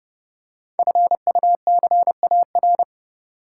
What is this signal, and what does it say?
Morse code, heard as a keyed tone, about 30 wpm, 700 Hz. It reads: FUCAR
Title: Code Groups - 5 Letter:Numbers - 30wpm